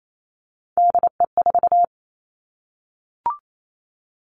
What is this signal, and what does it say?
Morse code, heard as a keyed tone, about 28 wpm, 700 Hz. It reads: DE4
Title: Three Challenging Letter-Number Combinations 28wpm